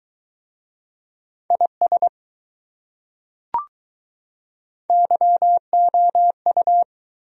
Morse code, heard as a keyed tone, about 22 wpm, 700 Hz. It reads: IS YOU